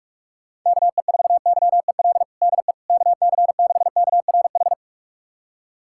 Morse code, heard as a keyed tone, about 45 wpm, 700 Hz. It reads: KE4YEL DE KC6KRH
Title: USA Callsign DE Callsign 45wpm